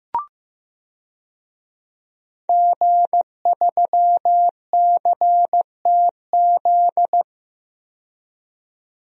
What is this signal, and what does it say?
Morse code, heard as a keyed tone, about 15 wpm, 700 Hz. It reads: G3CTZ
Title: DXCC-100 Call Signs - 5 Characters - 15wpm